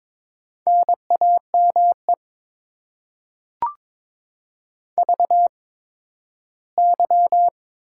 Morse code, heard as a keyed tone, about 22 wpm, spaced as for 13 wpm, 700 Hz. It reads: NAME VY
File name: Top 90 QSO Elements - 8x Character Spacing - 22wpm